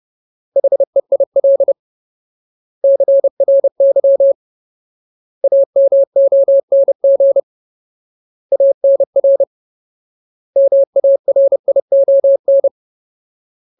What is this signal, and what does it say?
Morse code, heard as a keyed tone, about 30 wpm, 550 Hz. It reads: HEIL CRY AMONG ANR MARION
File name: Ultimate Word List / 30WPM Ultimate